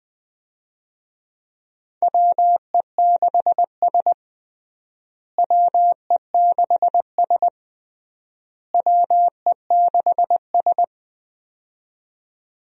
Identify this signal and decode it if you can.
Morse code, heard as a keyed tone, about 20 wpm, 700 Hz. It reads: WE6S WE6S WE6S
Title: USA Call Signs - 4 Characters - Repeated 3x - 20wpm